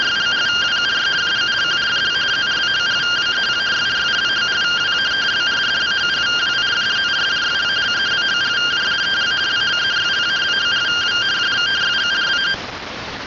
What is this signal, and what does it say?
Two-tone teleprinter signal, about 170 Hz shift.